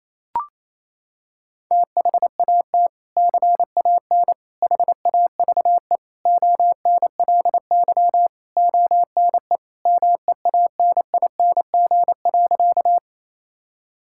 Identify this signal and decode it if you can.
Morse code, heard as a keyed tone, about 28 wpm, 700 Hz. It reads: THAT CAN HAVE ONLY ONE MEANING.
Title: Sentences from Top 1000 Words - Review Part 6 - 28wpm